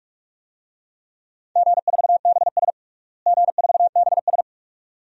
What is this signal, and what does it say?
Morse code, heard as a keyed tone, about 46 wpm, 700 Hz. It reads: G4BS G4BS
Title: DXCC-100 Call Signs - 4 Characters - Repeated 2x - 45wpm